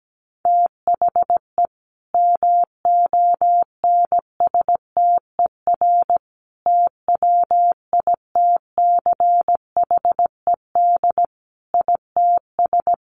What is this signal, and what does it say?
Morse code, heard as a keyed tone, about 17 wpm, 700 Hz. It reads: THE MONSTER TWITCHED ITS
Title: A Sound Of Thunder 17wpm